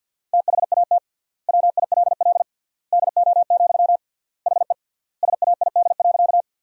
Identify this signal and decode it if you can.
Morse code, heard as a keyed tone, about 50 wpm, 700 Hz. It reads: THAT WILL DO, HE SAID.
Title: 3x Word-Spacing - Sentences from Top 100 Words - 50wpm